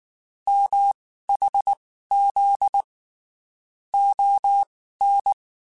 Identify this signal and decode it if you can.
Morse code, heard as a keyed tone, about 19 wpm, spaced as for 14 wpm, 800 Hz. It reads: MHZ ON
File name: (spaced) / QSOs Spaced-mp3s / kc7sac @18 sp